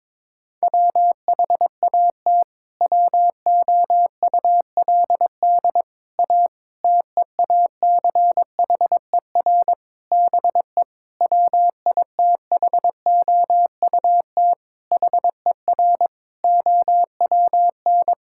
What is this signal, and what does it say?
Morse code, heard as a keyed tone, about 22 wpm, 700 Hz. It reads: WHAT WOULD A TEACHER BE WITHOUT HER OWN
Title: Sentences from Top 1000 Words - Review Part 6 - 22wpm